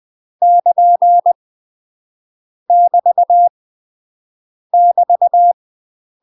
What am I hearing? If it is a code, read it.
Morse code, heard as a keyed tone, about 20 wpm, 700 Hz. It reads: <KN> <BT> <BT>
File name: UK MASTER QSOs TRAINING / 20WPM / UK MASTER QSOs 0056